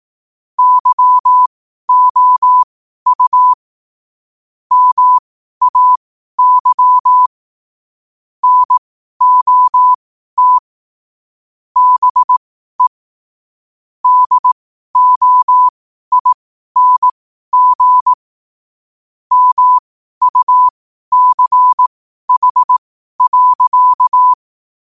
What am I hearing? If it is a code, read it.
Morse code, heard as a keyed tone, about 18 wpm, spaced as for 12 wpm, 1000 Hz. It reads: YOU MAY NOT BE DOING MUCH.
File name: qotdfiles / 12wpm